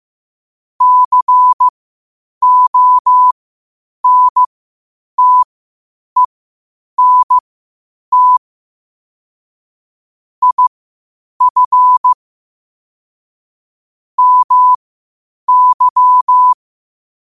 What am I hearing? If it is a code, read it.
Morse code, heard as a keyed tone, about 15 wpm, spaced as for 8 wpm, 1000 Hz. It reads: CONTENT IF MY
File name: qotdfiles / 8wpm